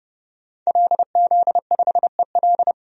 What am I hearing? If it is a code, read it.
Morse code, heard as a keyed tone, about 30 wpm, 700 Hz. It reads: LZ5EL
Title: DXCC-100 Call Signs - 5 Characters - 30wpm